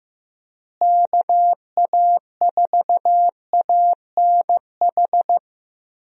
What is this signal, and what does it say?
Morse code, heard as a keyed tone, about 15 wpm, 700 Hz. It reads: KA4ANH
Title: USA Call Signs - 6 Characters - Repeated 2x - 15wpm